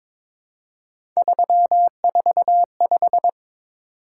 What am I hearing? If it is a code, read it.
Morse code, heard as a keyed tone, about 22 wpm, 700 Hz. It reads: 345
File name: practice / 22wpm / Three Letter-Number Combinations - 22wpm